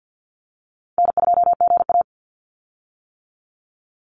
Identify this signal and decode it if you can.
Morse code, heard as a keyed tone, about 50 wpm, 700 Hz. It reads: N2GA
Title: Call Signs - 50wpm